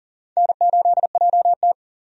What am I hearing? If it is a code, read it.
Morse code, heard as a keyed tone, about 38 wpm, 700 Hz. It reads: N8JT
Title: USA Call Signs - 4 Characters - 40wpm